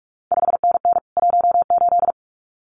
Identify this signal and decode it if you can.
Morse code, heard as a keyed tone, about 46 wpm, 700 Hz. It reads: 5NN 18
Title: CQ WW DX Contest Exchange Encore - 45wpm